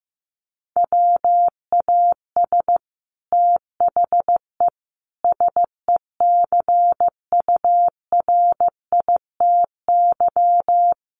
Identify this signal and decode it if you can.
Morse code, heard as a keyed tone, about 15 wpm, 700 Hz. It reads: WAS THE SECURITY
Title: Sentences from Top 1400 Words 15wpm